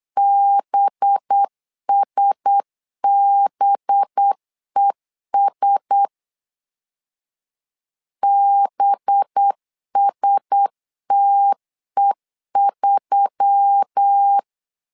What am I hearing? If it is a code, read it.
Morse code, heard as a keyed tone, about 8 wpm, 800 Hz. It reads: BSBES BSTE3
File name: LEZIONI / 5 LBVC / 5 LB 8wpm 2816